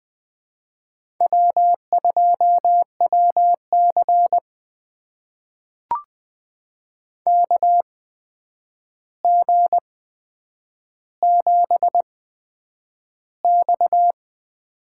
Morse code, heard as a keyed tone, about 20 wpm, spaced as for 9 wpm, 700 Hz. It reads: W2WC KG7X
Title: USA Call Signs - 4 Characters - 8x Character Spacing - 20wpm